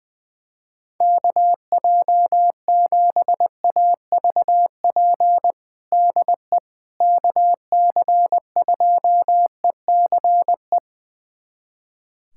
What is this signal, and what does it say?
Morse code, heard as a keyed tone, about 21 wpm, 700 Hz. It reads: KJ7AVP DE KC2ECE
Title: USA Callsign DE Callsign 20wpm